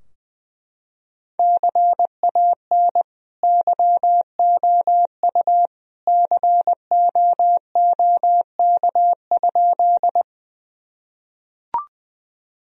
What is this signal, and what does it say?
Morse code, heard as a keyed tone, about 20 wpm, 700 Hz. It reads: CAN YOU COOK?
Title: Sentences from Top 1000 Words - Review Part 4 - 20wpm